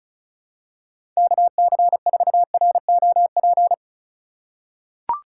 Morse code, heard as a keyed tone, about 34 wpm, 700 Hz. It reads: KC4ROP
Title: USA Call Signs - 6 Characters - 4x Character Spacing - 35wpm